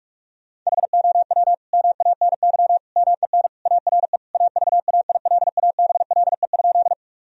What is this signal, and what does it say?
Morse code, heard as a keyed tone, about 42 wpm, 700 Hz. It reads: HOW MANY MEN ARE AVAILABLE?
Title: Sentences from Top 1000 Words - Review Part 3 - 45wpm